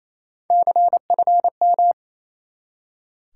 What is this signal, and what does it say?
Morse code, heard as a keyed tone, about 28 wpm, 700 Hz. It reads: CFM
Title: Top 90 QSO Elements 28wpm